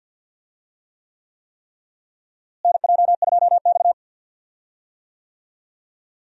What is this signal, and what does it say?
Morse code, heard as a keyed tone, about 50 wpm, 700 Hz. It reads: NJ2X
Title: USA Call Signs - 4 Characters - Repeated 3x - 50wpm